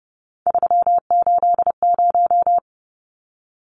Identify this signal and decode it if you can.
Morse code, heard as a keyed tone, about 30 wpm, 700 Hz. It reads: 380
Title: Numbers - 3-Digit 30wpm